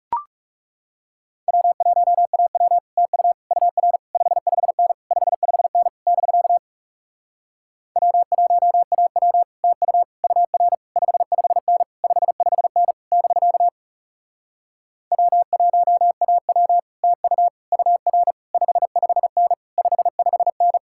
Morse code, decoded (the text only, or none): W1AW TU UR 55N 55N <BK> W1AW TU UR 55N 55N <BK> W1AW TU UR 55N 55N